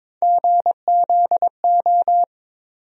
Morse code, heard as a keyed tone, about 22 wpm, 700 Hz. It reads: GZO
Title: USA Call Sign Postfixes - 3 Letters 22wpm